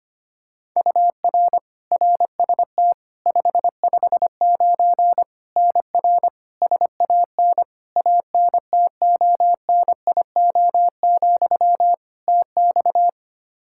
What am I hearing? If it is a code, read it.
Morse code, heard as a keyed tone, about 25 wpm, 700 Hz. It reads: UR RST 559 NR SAN ANTONIO, TX
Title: K1USN SST QSO Practice 25wpm